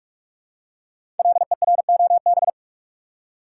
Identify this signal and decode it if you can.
Morse code, heard as a keyed tone, about 44 wpm, 700 Hz. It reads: LEROB
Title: Code Groups - 5 Letters - 45wpm